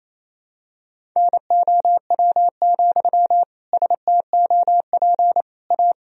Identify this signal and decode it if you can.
Morse code, heard as a keyed tone, about 28 wpm, 700 Hz. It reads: NOW, STOP A
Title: Sentences from Top 1000 Words - Review Part 1 - 28wpm